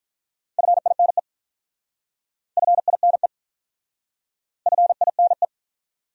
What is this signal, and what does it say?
Morse code, heard as a keyed tone, about 46 wpm, 700 Hz. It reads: FINE FINE FINE